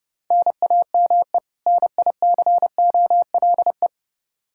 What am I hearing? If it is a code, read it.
Morse code, heard as a keyed tone, about 28 wpm, 700 Hz. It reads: NAME NICOLE
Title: Practice QSO - Exchange 1 - Beginning - 30wpm